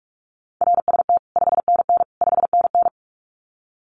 Morse code, heard as a keyed tone, about 44 wpm, 700 Hz. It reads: RST 5NN 5NN
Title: Common QSO Phrases Encore 45wpm